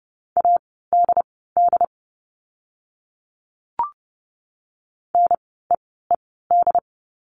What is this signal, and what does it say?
Morse code, heard as a keyed tone, about 30 wpm, spaced as for 17 wpm, 700 Hz. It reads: ADD NEED
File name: Lesson 17f - Letter u - Words 30wpm